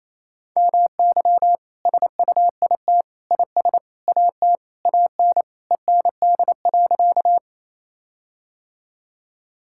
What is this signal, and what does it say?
Morse code, heard as a keyed tone, about 28 wpm, 700 Hz. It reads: MY SUIT IS AT AN END.
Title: Sentences from Top 1600 Words 28wpm